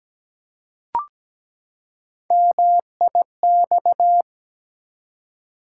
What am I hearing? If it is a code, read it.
Morse code, heard as a keyed tone, about 17 wpm, 700 Hz. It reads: MIX
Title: Three Letter Words - 17wpm